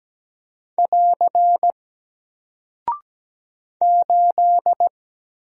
Morse code, heard as a keyed tone, about 17 wpm, 700 Hz. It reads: <AR> 8
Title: Single Character (Letter:Number:Punctuation:Prosign) Alt 17wpm